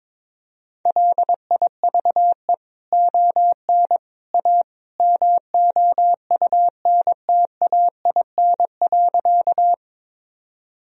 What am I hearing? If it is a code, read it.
Morse code, heard as a keyed tone, about 22 wpm, 700 Hz. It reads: LIVE ON A MOUNTAIN.
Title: Sentences from Top 1000 Words - Review Part 1 - 22wpm